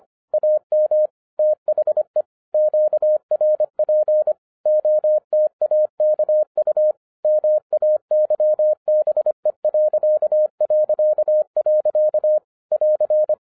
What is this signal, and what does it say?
Morse code, heard as a keyed tone, about 25 wpm, 610 Hz. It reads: AM THE QRP OTAKU MAYBE... <AR>
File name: 2011-07-mp3 / n24 QRP OTAKU 25wpm